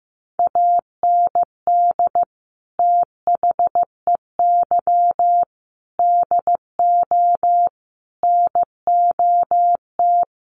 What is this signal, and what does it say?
Morse code, heard as a keyed tone, about 15 wpm, 700 Hz. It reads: AND THEY DO NOT
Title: Sets of 5 Words Encore2 15wpm